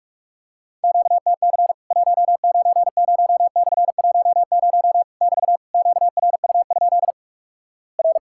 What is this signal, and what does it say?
Morse code, heard as a keyed tone, about 46 wpm, 700 Hz. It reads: QTC 190/10 <BT> QRU? R
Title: Worked All Europe DX Contest - QTC Exchange - 45wpm